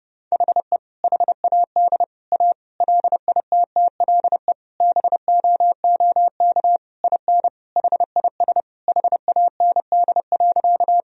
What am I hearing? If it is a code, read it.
Morse code, heard as a keyed tone, about 30 wpm, 700 Hz. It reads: HE HAD A LITTLE BOOK IN HIS HAND.
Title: Sentences from Top 1000 Words - Review Part 1 - 30wpm